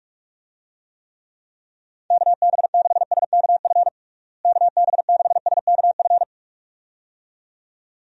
Morse code, heard as a keyed tone, about 41 wpm, 700 Hz. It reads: KB6SKF KB6SKF
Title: USA Call Signs - 6 Characters - Repeated 2x - 45wpm